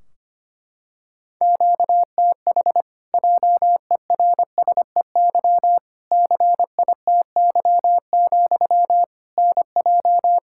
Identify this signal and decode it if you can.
Morse code, heard as a keyed tone, about 25 wpm, 700 Hz. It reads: QTH JERSEY CITY, NJ
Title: Practice QSO - Exchange 1 - Beginning - 25wpm